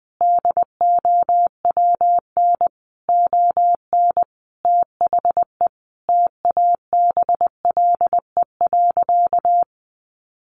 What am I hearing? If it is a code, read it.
Morse code, heard as a keyed tone, about 20 wpm, 700 Hz. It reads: DOWN ON THE TABLE.
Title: Sentences from Top 1100 Words 20wpm